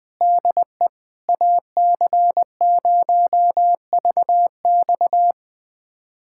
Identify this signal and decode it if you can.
Morse code, heard as a keyed tone, about 21 wpm, 700 Hz. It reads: DE AC0VX
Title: USA Callsign DE Callsign - 5 Characters - 20wpm